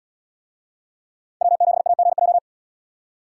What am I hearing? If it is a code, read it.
Morse code, heard as a keyed tone, about 74 wpm, 700 Hz. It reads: W7AGJ